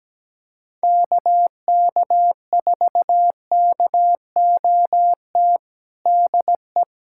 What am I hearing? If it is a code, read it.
Morse code, heard as a keyed tone, about 17 wpm, 700 Hz. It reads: KK4KOT DE
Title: USA Callsign DE Callsign 17wpm